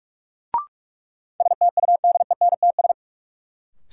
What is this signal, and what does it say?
Morse code, heard as a keyed tone, about 45 wpm, 700 Hz. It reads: STUDENTS
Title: Sets of 5 Words Encore - New Word Drill - 45wpm